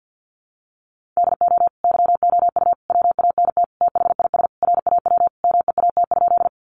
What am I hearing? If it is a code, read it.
Morse code, heard as a keyed tone, about 50 wpm, 700 Hz. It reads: DO YOU WANT THIS RAW MEAT?
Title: Sentences from Top 1500 Words 50wpm